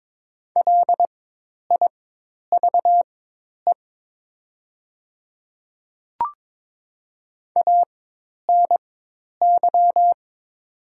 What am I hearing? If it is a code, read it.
Morse code, heard as a keyed tone, about 22 wpm, spaced as for 10 wpm, 700 Hz. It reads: LIVE ANY
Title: 4x Character Spacing - Top 200 Words - 22wpm